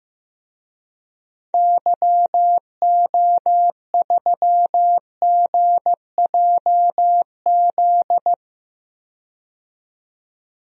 Morse code, heard as a keyed tone, about 15 wpm, 700 Hz. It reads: YO3GJZ
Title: DXCC-100 Call Signs - 6 Characters - 15wpm